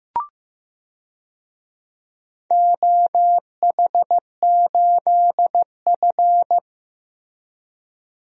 Morse code, heard as a keyed tone, about 15 wpm, 700 Hz. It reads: OH8F
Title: DXCC-100 Call Signs - 4 Characters - Repeated 3x - 15wpm